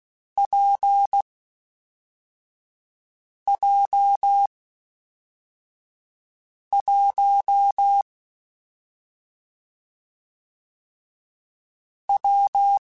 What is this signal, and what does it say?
Morse code, heard as a keyed tone, about 16 wpm, 800 Hz. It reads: P J 1 W
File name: Difficult Combos / Difficult Combos-mp3s / WPJ WPJ1 @5